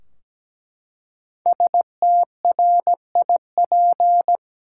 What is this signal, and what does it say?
Morse code, heard as a keyed tone, about 17 wpm, 700 Hz. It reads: STRIP
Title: Five Letter Words - 17wpm